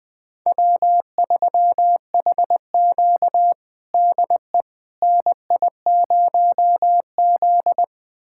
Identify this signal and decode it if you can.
Morse code, heard as a keyed tone, about 21 wpm, 700 Hz. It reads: W3HQ DE NI0Z
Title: USA Callsign DE Callsign - 4 Characters - 20wpm